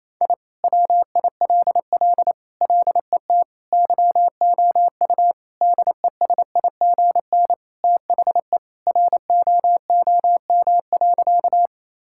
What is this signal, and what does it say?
Morse code, heard as a keyed tone, about 28 wpm, 700 Hz. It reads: I WILL LET YOU DESIGN THE ROOM.
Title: Sentences from Top 1000 Words - Review Part 5 - 28wpm